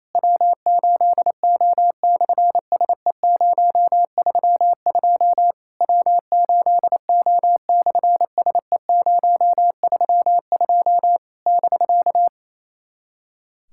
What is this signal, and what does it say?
Morse code, heard as a keyed tone, about 28 wpm, 700 Hz. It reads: W8O/SE032 W8O/SE032 <BK>
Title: SOTA QSO Practice 28wpm